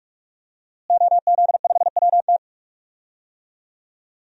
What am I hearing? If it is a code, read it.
Morse code, heard as a keyed tone, about 46 wpm, 700 Hz. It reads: OZ5WT
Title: DXCC-100 Call Signs - 5 Characters - Repeated 3x - 45wpm